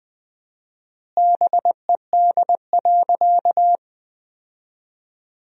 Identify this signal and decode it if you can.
Morse code, heard as a keyed tone, about 21 wpm, 700 Hz. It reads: BED.
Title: 4x Word-Spacing - Sentences from Top 500 Words - 20wpm